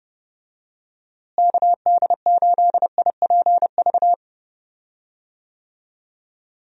Morse code, heard as a keyed tone, about 30 wpm, 700 Hz. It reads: KD8IPV
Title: USA Call Signs - 6 Characters - Repeated 2x - 30wpm